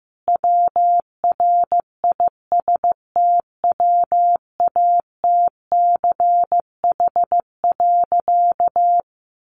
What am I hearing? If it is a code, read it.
Morse code, heard as a keyed tone, about 15 wpm, 700 Hz. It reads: WRISTWATCH.
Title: A Sound Of Thunder 15wpm